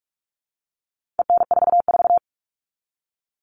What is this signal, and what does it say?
Morse code, heard as a keyed tone, about 44 wpm, 700 Hz. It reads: EN44
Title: All Cut Numbers - 4-Digit 45wpm